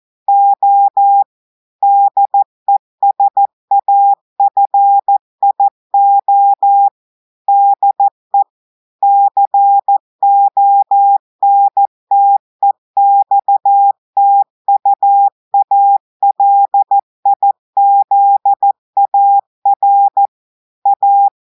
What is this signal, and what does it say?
Morse code, heard as a keyed tone, about 14 wpm, 800 Hz. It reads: O DESAFIO DE CONTEXTUALIZAR A
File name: audios / portugais / portugais texte en clair 1 vitesse 14 mots minutes